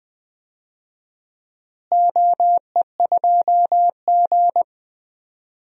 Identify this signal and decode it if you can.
Morse code, heard as a keyed tone, about 20 wpm, 700 Hz. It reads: OE2G
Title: DXCC-100 Call Signs - 4 Characters - Repeated 3x - 20wpm